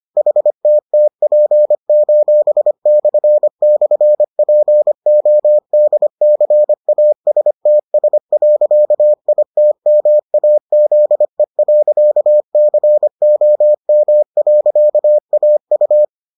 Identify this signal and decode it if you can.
Morse code, heard as a keyed tone, about 25 wpm, 600 Hz. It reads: HTTP://PODCASTS.ITMAZE.COM.AU